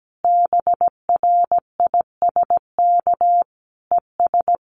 Morse code, heard as a keyed tone, about 17 wpm, 700 Hz. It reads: BRISK ES
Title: Common QSO Phrases Encore 17wpm